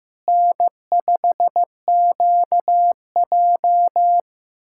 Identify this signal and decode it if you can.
Morse code, heard as a keyed tone, about 15 wpm, 700 Hz. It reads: N5QJ
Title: K1USN SST QSO Practice - 15wpm